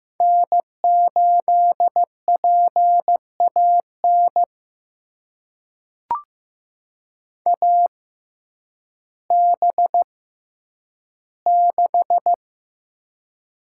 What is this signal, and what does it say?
Morse code, heard as a keyed tone, about 15 wpm, spaced as for 9 wpm, 700 Hz. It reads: N8PAN AB6
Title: USA Call Signs - 5 Characters - 6x Character Spacing - 15wpm